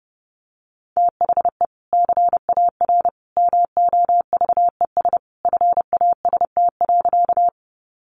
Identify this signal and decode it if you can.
Morse code, heard as a keyed tone, about 30 wpm, 700 Hz. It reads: THE CAR MOVES FAST.
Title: Easy Phrases 30wpm